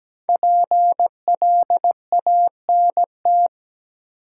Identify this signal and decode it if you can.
Morse code, heard as a keyed tone, about 17 wpm, 700 Hz. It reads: PLANT
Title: Five Letter Words - 17wpm